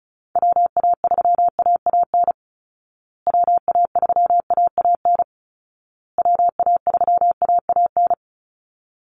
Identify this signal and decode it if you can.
Morse code, heard as a keyed tone, about 34 wpm, 700 Hz. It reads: WA3AAN WA3AAN WA3AAN
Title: Call Signs Repeated 2x - 35wpm